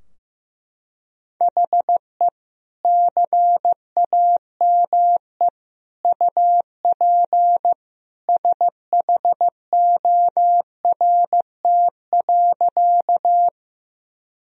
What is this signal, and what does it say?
Morse code, heard as a keyed tone, about 15 wpm, 700 Hz. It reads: HE CAME UP SHORT.
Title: Sentences from Top 1000 Words - Review Part 2 - 15wpm